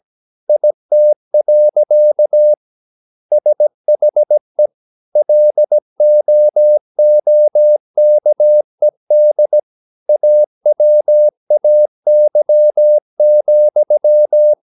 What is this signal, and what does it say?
Morse code, heard as a keyed tone, about 17 wpm, 600 Hz. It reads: IT. SHE LOOKED AWAY,
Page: Romanen Never af Ken Follet på engelsk.